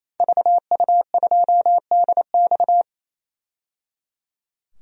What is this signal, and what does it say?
Morse code, heard as a keyed tone, about 27 wpm, 700 Hz. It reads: VU2DX
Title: DXCC-100 Call Signs - 5 Characters - 28wpm